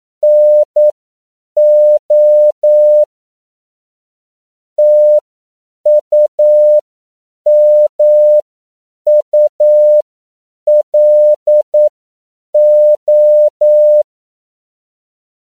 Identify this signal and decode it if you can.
Morse code, heard as a keyed tone, about 9 wpm, spaced as for 7 wpm, 600 Hz. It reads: NO TUMULO
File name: CW / 10PPM